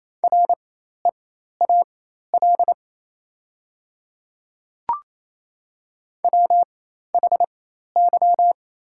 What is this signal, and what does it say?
Morse code, heard as a keyed tone, about 28 wpm, spaced as for 13 wpm, 700 Hz. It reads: REAL WHY
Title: 4x Character Spacing - Top 300 Words - 28wpm